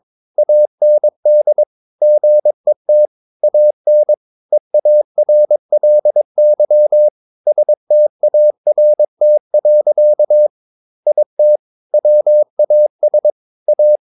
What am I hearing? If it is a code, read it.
Morse code, heard as a keyed tone, about 22 wpm, 600 Hz. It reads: AND GET AN EARLY START. IT WAS A
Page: Romanen Never af Ken Follet på engelsk.